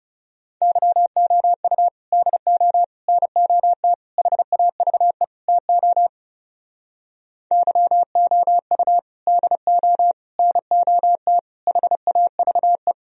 Morse code, heard as a keyed tone, about 37 wpm, 700 Hz. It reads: YOU DO NOT HAVE TO YOU DO NOT HAVE
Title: Speed-Racing - Sets of 5 Words 30wpm